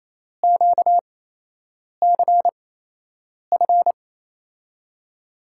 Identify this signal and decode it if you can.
Morse code, heard as a keyed tone, about 28 wpm, spaced as for 8 wpm, 700 Hz. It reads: QCF
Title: USA Call Signs - 6 Characters - 8x Character Spacing - 28wpm